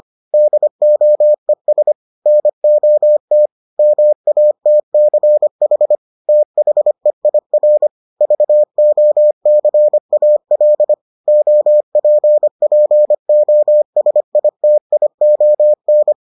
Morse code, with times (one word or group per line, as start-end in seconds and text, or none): 0.3-1.9 DOES
2.3-3.5 NOT
3.8-6.0 MATCH
6.3-7.9 THEIR
8.2-10.9 VOCAL
11.3-16.1 OPPOSITION